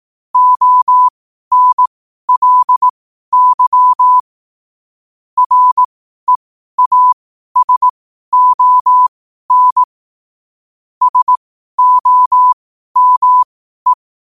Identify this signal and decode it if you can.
Morse code, heard as a keyed tone, about 18 wpm, spaced as for 12 wpm, 1000 Hz. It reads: ONLY REASON SOME
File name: qotdfiles / 12wpm